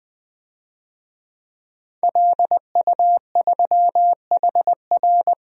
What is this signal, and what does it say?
Morse code, heard as a keyed tone, about 20 wpm, 700 Hz. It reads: LU3HR
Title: DXCC-100 Call Signs - 5 Characters - 20wpm